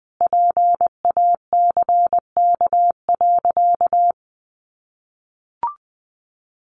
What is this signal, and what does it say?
Morse code, heard as a keyed tone, about 19 wpm, 700 Hz. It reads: PACK.
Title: Sentences from Top 1300 Words 20wpm